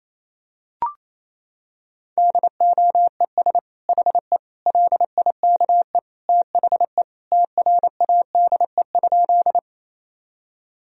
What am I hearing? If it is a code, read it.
Morse code, heard as a keyed tone, about 28 wpm, 700 Hz. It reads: DOES HE LIKE THE TRADE?
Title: Sentences from Top 1000 Words - Review Part 4 - 28wpm